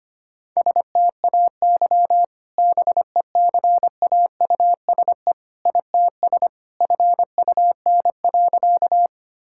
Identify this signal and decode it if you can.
Morse code, heard as a keyed tone, about 25 wpm, 700 Hz. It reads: STAY BECAUSE ITS FUN.
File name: practice / 25wpm / Sentences with 2-Word Phrases - 25wpm